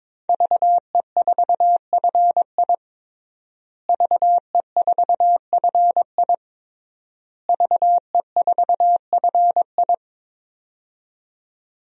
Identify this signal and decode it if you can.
Morse code, heard as a keyed tone, about 22 wpm, 700 Hz. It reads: VE4FI VE4FI VE4FI
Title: DXCC-100 Call Signs - 5 Characters - Repeated 3x - 22wpm